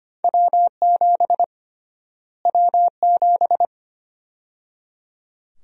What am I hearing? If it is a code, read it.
Morse code, heard as a keyed tone, about 25 wpm, 700 Hz. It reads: W7 W7
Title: USA Call Sign Prefixes - Repeated 2x - 25wpm